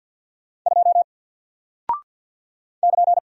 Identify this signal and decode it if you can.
Morse code, heard as a keyed tone, about 48 wpm, 700 Hz. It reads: 2 <KN>
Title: Single Character (Letter:Number:Punctuation:Prosign) 50wpm